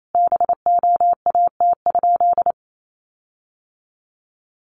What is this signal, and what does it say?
Morse code, heard as a keyed tone, about 28 wpm, 700 Hz. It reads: BOAT?
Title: Sentences from Top 1300 Words 28wpm